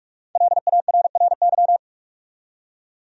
Morse code, heard as a keyed tone, about 46 wpm, 700 Hz. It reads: LARRY
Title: USA Common Names - Max 5 Letters - 45wpm